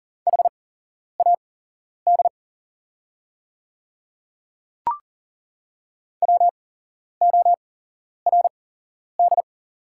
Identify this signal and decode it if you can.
Morse code, heard as a keyed tone, about 40 wpm, spaced as for 11 wpm, 700 Hz. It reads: HAD WORD
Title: 8x Character Spacing - Top 100 Words - 40wpm